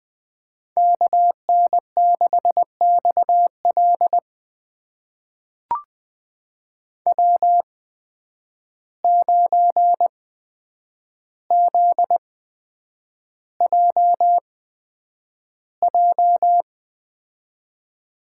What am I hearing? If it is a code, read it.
Morse code, heard as a keyed tone, about 20 wpm, spaced as for 9 wpm, 700 Hz. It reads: KN6XL W9ZJJ
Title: USA Call Signs - 5 Characters - 8x Character Spacing - 20wpm